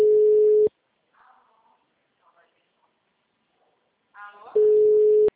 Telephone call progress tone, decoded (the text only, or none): ringback tone